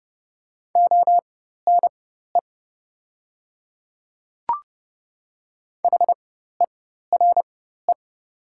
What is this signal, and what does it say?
Morse code, heard as a keyed tone, about 30 wpm, spaced as for 14 wpm, 700 Hz. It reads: ONE HERE
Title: 4x Character Spacing - Top 200 Words - 30wpm